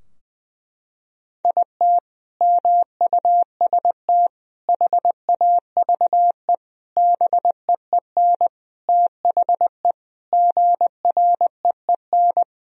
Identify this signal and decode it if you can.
Morse code, heard as a keyed tone, about 20 wpm, 700 Hz. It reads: IT MUST HAVE BEEN THE GREEN
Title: Sentences from Top 1000 Words - Review Part 2 - 20wpm